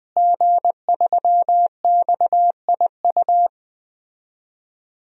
Morse code, heard as a keyed tone, about 20 wpm, 700 Hz. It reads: G3XIU
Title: DXCC-100 Call Signs - 5 Characters - Repeated 2x - 20wpm